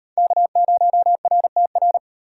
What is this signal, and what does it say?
Morse code, heard as a keyed tone, about 37 wpm, 700 Hz. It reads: K0RTR